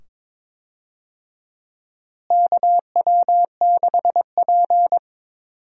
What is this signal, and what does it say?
Morse code, heard as a keyed tone, about 22 wpm, 700 Hz. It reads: KW6P
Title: USA Call Signs - 4 Characters - Repeated 2x - 22wpm